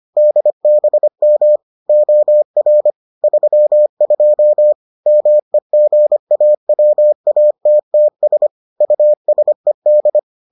Morse code, heard as a keyed tone, about 25 wpm, 600 Hz. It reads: DBM OR 32 MEGAWATTS USED